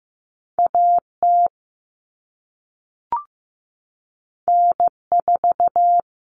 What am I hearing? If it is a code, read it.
Morse code, heard as a keyed tone, about 15 wpm, 700 Hz. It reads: AT N4
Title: All Cut Numbers - 2-Digit 15wpm